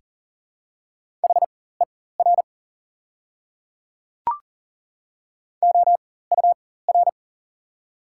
Morse code, heard as a keyed tone, about 40 wpm, spaced as for 19 wpm, 690 Hz. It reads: HER OUR
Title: 4x Character Spacing - Top 200 Words - 40wpm